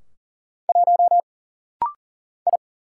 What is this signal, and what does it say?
Morse code, heard as a keyed tone, about 39 wpm, 700 Hz. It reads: 1 I
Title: Single Letter-Number Alt - Rapid-Fire 40wpm